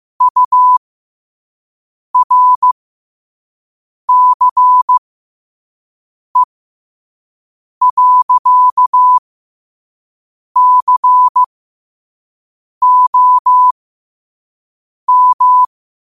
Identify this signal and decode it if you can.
Morse code, heard as a keyed tone, about 15 wpm, spaced as for 5 wpm, 1000 Hz. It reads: URCE.COM
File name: qotdfiles / 5wpm